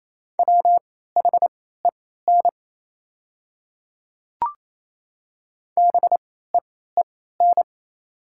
Morse code, heard as a keyed tone, about 28 wpm, spaced as for 16 wpm, 700 Hz. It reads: WHEN BEEN
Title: 3x Character Spacing - Top 100 Words - 28wpm